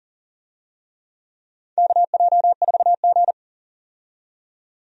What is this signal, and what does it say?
Morse code, heard as a keyed tone, about 37 wpm, 700 Hz. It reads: KJ4G
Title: USA Call Signs - 4 Characters - 40wpm